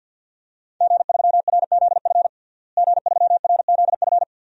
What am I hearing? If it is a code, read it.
Morse code, heard as a keyed tone, about 52 wpm, 700 Hz. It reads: G3RZF G3RZF
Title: DXCC-100 Call Signs - 5 Characters - Repeated 2x - 50wpm